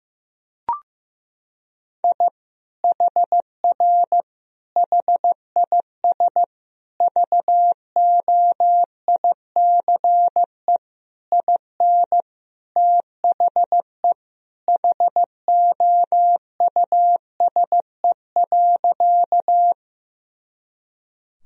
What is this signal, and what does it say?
Morse code, heard as a keyed tone, about 15 wpm, 700 Hz. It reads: I HR HIS VOICE IN THE HOUSE.
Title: Sentences from Top 500 Words and Abbreviations 15wpm